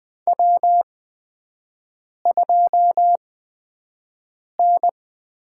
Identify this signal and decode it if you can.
Morse code, heard as a keyed tone, about 20 wpm, spaced as for 5 wpm, 700 Hz. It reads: W2N
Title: USA Call Signs - 5 Characters - 8x Character Spacing - 20wpm